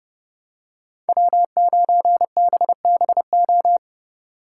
Morse code, heard as a keyed tone, about 31 wpm, 700 Hz. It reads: W9BBO
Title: USA Call Signs - 5 Characters - 3x Character Spacing - 30wpm